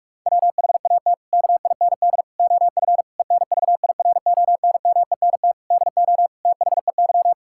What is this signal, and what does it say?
Morse code, heard as a keyed tone, about 46 wpm, 700 Hz. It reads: WHAT KIND OF ENVIRONMENT DO THEY
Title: Sentences from Top 1000 Words - Review Part 5 - 45wpm